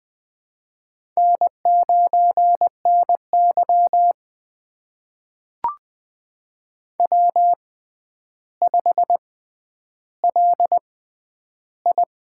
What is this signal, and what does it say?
Morse code, heard as a keyed tone, about 20 wpm, spaced as for 10 wpm, 700 Hz. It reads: N9NY W5LI
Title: USA Call Signs - 6x Character Spacing - 20wpm